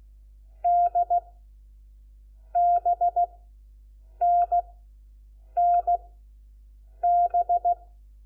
Morse code, heard as a keyed tone, about 16 wpm, spaced as for 7 wpm, 700 Hz. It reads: D BNNB
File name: Lecciones / CLASE Nº4